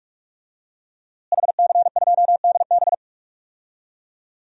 Morse code, heard as a keyed tone, about 43 wpm, 700 Hz. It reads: HC2DB
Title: DXCC-100 Call Signs - 5 Characters - Repeated 2x - 45wpm